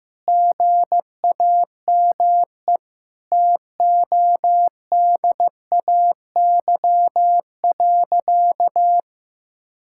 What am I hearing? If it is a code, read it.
Morse code, heard as a keyed tone, about 15 wpm, 700 Hz. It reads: GAME TODAY.
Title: Sentences from Top 1600 Words 15wpm